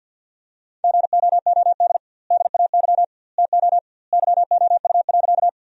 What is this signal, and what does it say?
Morse code, heard as a keyed tone, about 51 wpm, 700 Hz. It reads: GOOD DAY TO YOU.
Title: 2x Word-Spacing - Sentences from Top 200 Words - 50wpm